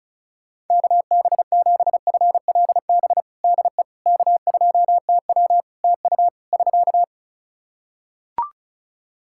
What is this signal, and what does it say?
Morse code, heard as a keyed tone, about 36 wpm, 700 Hz. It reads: KB7FLB DE K2TW TU <SK>
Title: Practice QSO - Exchange 4 - Ending - 35wpm